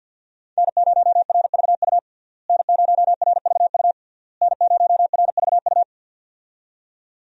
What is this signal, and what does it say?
Morse code, heard as a keyed tone, about 52 wpm, 700 Hz. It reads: N0RVU N0RVU N0RVU
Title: USA Call Signs - 5 Characters - Repeated 3x - 50wpm